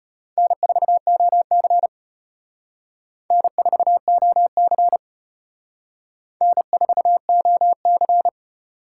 Morse code, heard as a keyed tone, about 32 wpm, 700 Hz. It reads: N4OC N4OC N4OC